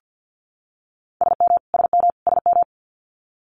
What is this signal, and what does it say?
Morse code, heard as a keyed tone, about 50 wpm, 700 Hz. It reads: SM SM SM
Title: Multi-letter Phonemes Repeated 3x 50wpm